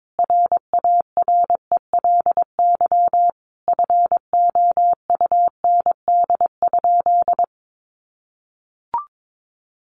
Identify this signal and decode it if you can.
Morse code, heard as a keyed tone, about 22 wpm, 700 Hz. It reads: RARELY FOUND?
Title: Sentences from Top 1400 Words 22wpm